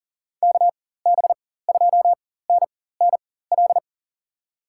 Morse code, heard as a keyed tone, about 40 wpm, spaced as for 19 wpm, 700 Hz. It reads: KB2NNL
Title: USA Call Signs - 6 Characters - 4x Character Spacing - 40wpm